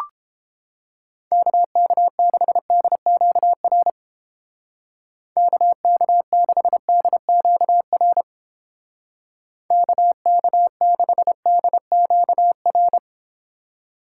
Morse code, heard as a keyed tone, about 29 wpm, 700 Hz. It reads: KK6DQR KK6DQR KK6DQR